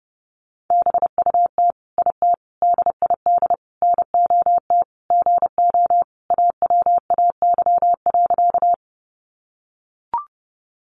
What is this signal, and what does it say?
Morse code, heard as a keyed tone, about 30 wpm, 700 Hz. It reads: BUT IT DID NOT GO AWAY.
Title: Sentences from Top 600 Words 30wpm